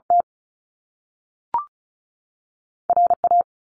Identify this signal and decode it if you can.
Morse code, heard as a keyed tone, about 36 wpm, 700 Hz. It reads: T R A
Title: Top Two Letter Combinations in Words 35wpm